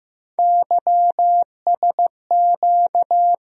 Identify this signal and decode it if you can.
Morse code, heard as a keyed tone, about 15 wpm, 700 Hz. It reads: YSQ
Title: USA Call Sign Postfixes - 3 Letters 15wpm